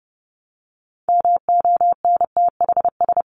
Morse code, heard as a keyed tone, about 30 wpm, 700 Hz. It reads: MONTHS
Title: ES:S Suffix - Top 100 Words with Base Words 30wpm